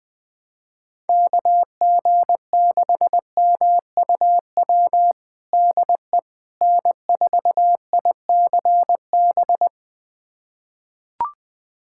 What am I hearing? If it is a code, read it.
Morse code, heard as a keyed tone, about 21 wpm, 700 Hz. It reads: KG6MUW DE N4ICB
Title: USA Callsign DE Callsign 20wpm